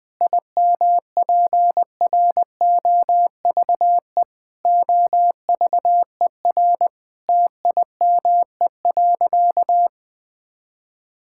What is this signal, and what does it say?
Morse code, heard as a keyed tone, about 20 wpm, 700 Hz. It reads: IMPROVE OVER TIME.
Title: Sentences from Top 1000 Words - Review Part 5 - 20wpm